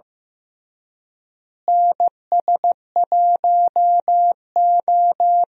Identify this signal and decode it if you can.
Morse code, heard as a keyed tone, about 15 wpm, 700 Hz. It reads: NS1O
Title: USA Call Signs - 4 Characters - Repeated 3x - 15wpm